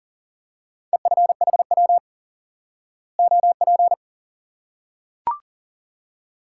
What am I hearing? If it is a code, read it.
Morse code, heard as a keyed tone, about 40 wpm, 700 Hz. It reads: EFHW OP
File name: IQE Course 3 - Lesson 89 - YAGI - Two Elements - 40wpm